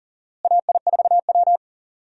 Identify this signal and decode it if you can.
Morse code, heard as a keyed tone, about 40 wpm, 700 Hz. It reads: AI4W
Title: USA Call Signs - 4 Characters - Repeated 2x - 40wpm